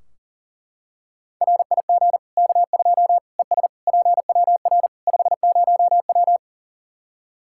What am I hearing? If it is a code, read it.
Morse code, heard as a keyed tone, about 37 wpm, 700 Hz. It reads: RIG K2 ES PWR 50W
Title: Practice QSO - Exchange 2 - Info - 40wpm